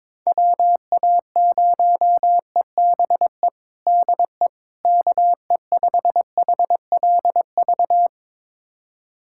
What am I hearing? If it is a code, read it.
Morse code, heard as a keyed tone, about 22 wpm, 700 Hz. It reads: WA0EBE DE KE5HLV
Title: USA Callsign DE Callsign - 6 Characters - 22wpm